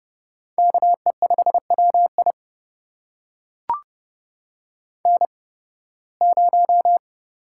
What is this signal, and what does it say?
Morse code, heard as a keyed tone, about 30 wpm, spaced as for 20 wpm, 700 Hz. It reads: KE5WI N0
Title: USA Call Signs - 5 Characters - 8x Character Spacing - 30wpm